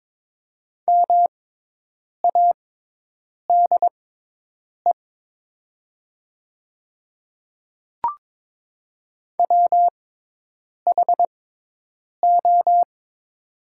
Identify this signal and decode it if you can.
Morse code, heard as a keyed tone, about 22 wpm, spaced as for 8 wpm, 700 Hz. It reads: MADE WHO
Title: 6x Character Spacing - Top 200 Words - 22wpm